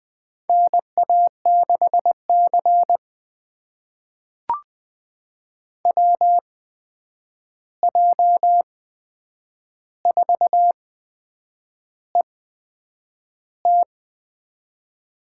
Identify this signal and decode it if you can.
Morse code, heard as a keyed tone, about 20 wpm, spaced as for 8 wpm, 700 Hz. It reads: NA6C WJ4ET
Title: USA Call Signs - 8x Character Spacing - 20wpm